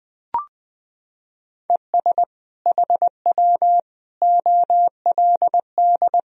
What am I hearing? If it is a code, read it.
Morse code, heard as a keyed tone, about 20 wpm, 700 Hz. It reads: ES HW OLD
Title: Sentences from Top 200 Words and Abbreviations 20wpm